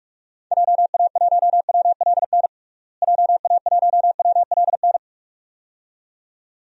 Morse code, heard as a keyed tone, about 46 wpm, 700 Hz. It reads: JA1WLN JA1WLN
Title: DXCC-100 Call Signs - 6 Characters - Repeated 2x - 45wpm